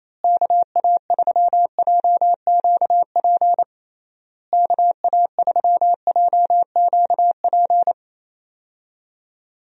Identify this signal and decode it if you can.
Morse code, heard as a keyed tone, about 28 wpm, 700 Hz. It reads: KA3JQP KA3JQP
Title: USA Call Signs - 6 Characters - Repeated 3x - 28wpm